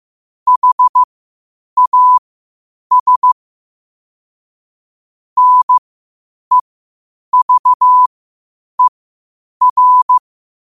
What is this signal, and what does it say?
Morse code, heard as a keyed tone, about 15 wpm, spaced as for 8 wpm, 1000 Hz. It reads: HAS NEVER
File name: qotdfiles / 8wpm